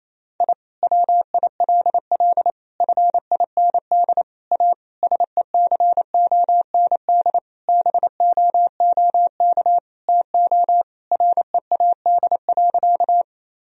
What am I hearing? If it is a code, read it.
Morse code, heard as a keyed tone, about 28 wpm, 700 Hz. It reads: I WILL FIND A SECOND BOOK TO READ.
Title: Sentences from Top 1000 Words - Review Part 1 - 28wpm